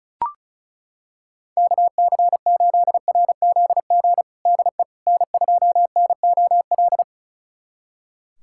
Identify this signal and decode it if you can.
Morse code, heard as a keyed tone, about 36 wpm, 700 Hz. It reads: KC8RZG DE N2NOL
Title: USA Callsign DE Callsign 35wpm